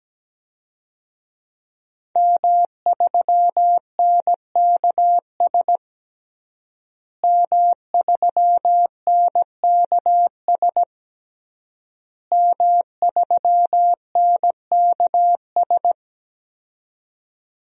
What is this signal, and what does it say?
Morse code, heard as a keyed tone, about 17 wpm, 700 Hz. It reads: M3NKS M3NKS M3NKS
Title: DXCC-100 Call Signs - 5 Characters - Repeated 3x - 17wpm